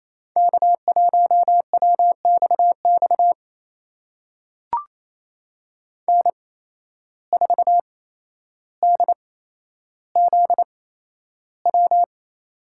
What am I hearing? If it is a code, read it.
Morse code, heard as a keyed tone, about 28 wpm, spaced as for 12 wpm, 700 Hz. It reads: K1WXX N4DZW
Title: USA Call Signs - 5 Characters - 8x Character Spacing - 28wpm